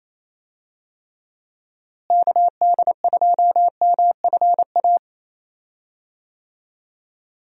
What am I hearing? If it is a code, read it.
Morse code, heard as a keyed tone, about 28 wpm, 700 Hz. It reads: KD2MFA
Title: USA Call Signs - 6 Characters - Repeated 3x - 28wpm